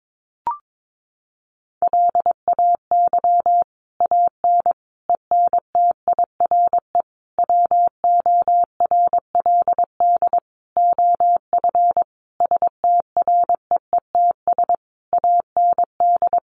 Morse code, heard as a keyed tone, about 22 wpm, 700 Hz. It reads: LAY AN ENTIRE WORLD OF STREETS AND
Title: A Sound Of Thunder 22wpm